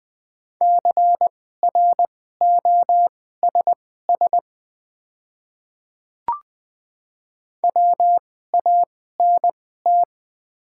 Morse code, heard as a keyed tone, about 20 wpm, spaced as for 14 wpm, 700 Hz. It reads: CROSS WANT
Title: 2x Character Spacing - Top 300 Words - 20wpm